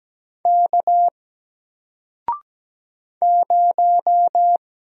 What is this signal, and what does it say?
Morse code, heard as a keyed tone, about 17 wpm, 700 Hz. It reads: K 0
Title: Single Letter-Number 17wpm